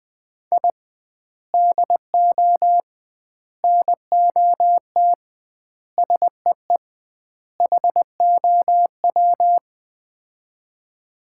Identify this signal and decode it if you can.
Morse code, heard as a keyed tone, about 20 wpm, 700 Hz. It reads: I DO NOT SEE HOW
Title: 2x Word-Spacing - Sets of 5 Words - 20wpm